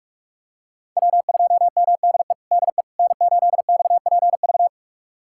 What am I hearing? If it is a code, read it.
Morse code, heard as a keyed tone, about 47 wpm, 700 Hz. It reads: W2MDE DE N8XPV
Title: USA Callsign DE Callsign - 5 Characters - 45wpm